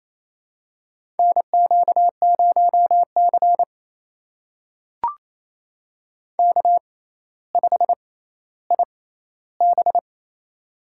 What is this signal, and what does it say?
Morse code, heard as a keyed tone, about 28 wpm, spaced as for 14 wpm, 700 Hz. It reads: NQ0C K5IB
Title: USA Call Signs - 4 Characters - 6x Character Spacing - 28wpm